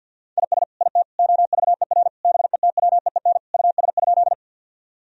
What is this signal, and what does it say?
Morse code, heard as a keyed tone, about 51 wpm, 700 Hz. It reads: IS IT OVER BETWEEN US?
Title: Sentences from Top 1000 Words - Review Part 1 - 50wpm